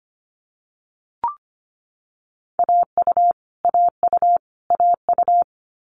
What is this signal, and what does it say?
Morse code, heard as a keyed tone, about 25 wpm, 700 Hz. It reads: AU AU AU
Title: Multi-letter Phonemes Repeated 3x 25wpm